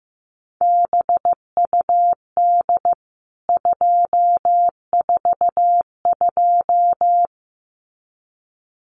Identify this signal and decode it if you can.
Morse code, heard as a keyed tone, about 15 wpm, 700 Hz. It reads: BUD 242
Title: CWT Exchange 15wpm